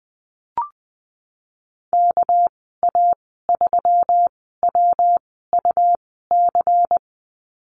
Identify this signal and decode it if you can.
Morse code, heard as a keyed tone, about 20 wpm, spaced as for 15 wpm, 700 Hz. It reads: KA3WUC
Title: 231 Lesson 37g - Letter x - Call Signs 20wpm